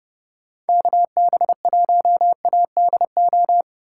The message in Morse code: KB1ADO

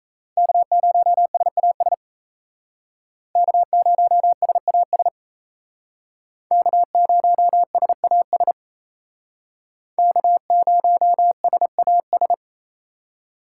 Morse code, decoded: K0SAS K0SAS K0SAS K0SAS

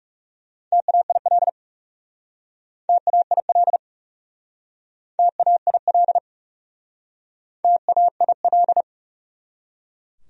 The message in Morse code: TAIL TAIL TAIL TAIL

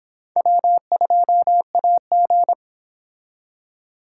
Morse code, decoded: W2AG

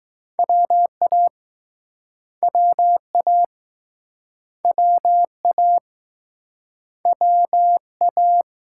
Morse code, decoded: WA WA WA WA

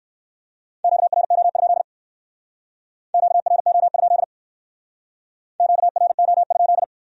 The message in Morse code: QRO? QRO? QRO?

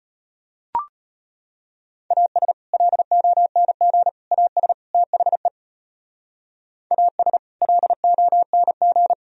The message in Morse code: AS LONG AS THE AS LONG

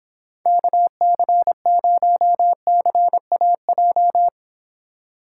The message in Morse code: KC0CAJ

